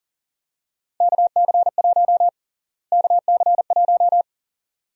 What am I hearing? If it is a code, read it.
Morse code, heard as a keyed tone, about 40 wpm, 700 Hz. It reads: KC1 KC1
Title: USA Call Sign Prefixes - Repeated 2x - 40wpm